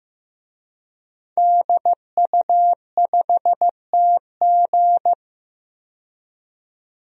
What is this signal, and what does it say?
Morse code, heard as a keyed tone, about 15 wpm, 700 Hz. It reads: DU5TG
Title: DXCC-100 Call Signs - 5 Characters - 15wpm